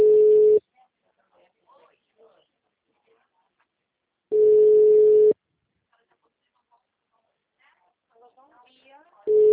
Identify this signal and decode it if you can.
Ringback tone (Europe, 425 Hz), with speech.